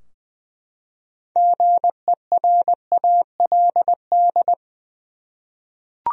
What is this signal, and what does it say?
Morse code, heard as a keyed tone, about 20 wpm, 700 Hz. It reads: GERALD
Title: USA Common Names - 20wpm